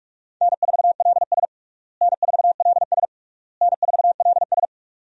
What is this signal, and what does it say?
Morse code, heard as a keyed tone, about 46 wpm, 700 Hz. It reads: N4LS N4LS N4LS
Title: USA Call Signs - 4 Characters - Repeated 3x - 45wpm